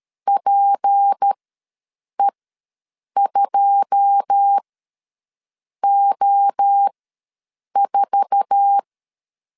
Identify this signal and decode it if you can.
Morse code, heard as a keyed tone, about 13 wpm, spaced as for 7 wpm, 800 Hz. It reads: PE2O4
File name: LEZIONI / 10 SIGNS / 10B INT ws 12wpm 3527